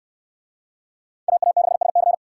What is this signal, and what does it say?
Morse code, heard as a keyed tone, about 63 wpm, 700 Hz. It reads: AA6SC